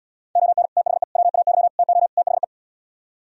Morse code, heard as a keyed tone, about 77 wpm, 700 Hz. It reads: ON THE WAY TO THE